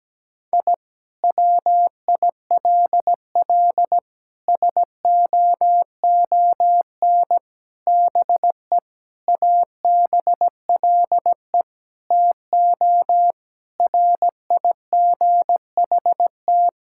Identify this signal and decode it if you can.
Morse code, heard as a keyed tone, about 17 wpm, 700 Hz. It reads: I WILL SOON BE ABLE TO RIGHT
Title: Sentences from Top 1000 Words - Review Part 6 - 17wpm